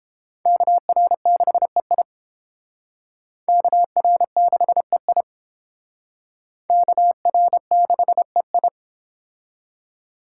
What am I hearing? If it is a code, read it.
Morse code, heard as a keyed tone, about 31 wpm, 700 Hz. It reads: KR6EI KR6EI KR6EI